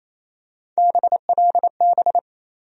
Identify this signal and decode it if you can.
Morse code, heard as a keyed tone, about 27 wpm, 700 Hz. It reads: BLB
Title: USA Call Sign Postfixes - 3 Letters 28wpm